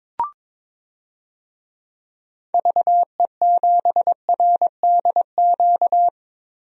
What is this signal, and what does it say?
Morse code, heard as a keyed tone, about 22 wpm, 700 Hz. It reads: VE7RDQ
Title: DXCC-100 Call Signs - 6 Characters - Repeated 3x - 22wpm